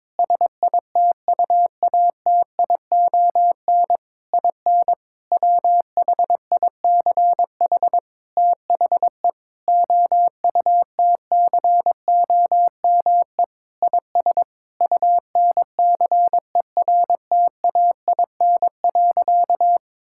Morse code, heard as a keyed tone, about 22 wpm, 700 Hz. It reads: SITUATION IN WHICH THE OUTCOME IS UNCERTAIN.
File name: practice / 22wpm / Sentences with 3-Word Phrases - 22wpm